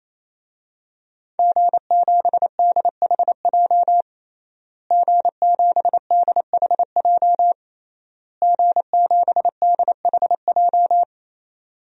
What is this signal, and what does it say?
Morse code, heard as a keyed tone, about 28 wpm, 700 Hz. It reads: G7DHJ G7DHJ G7DHJ
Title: DXCC-100 Call Signs - 5 Characters - Repeated 3x - 28wpm